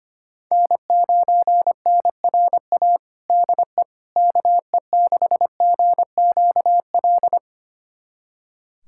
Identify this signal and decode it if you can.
Morse code, heard as a keyed tone, about 25 wpm, 700 Hz. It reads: N9NRA DE KE6GQL
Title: USA Callsign DE Callsign 25wpm